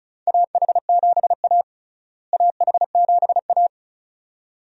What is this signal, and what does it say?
Morse code, heard as a keyed tone, about 34 wpm, 700 Hz. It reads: AH7A AH7A
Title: USA Call Signs - 4 Characters - Repeated 2x - 35wpm